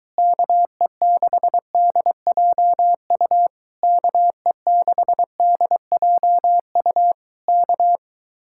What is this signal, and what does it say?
Morse code, heard as a keyed tone, about 24 wpm, 700 Hz. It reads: KE6DJU KE6DJU K